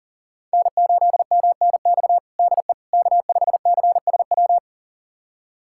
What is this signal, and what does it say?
Morse code, heard as a keyed tone, about 37 wpm, 700 Hz. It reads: N8MNX DE K5CSW
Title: USA Callsign DE Callsign 40wpm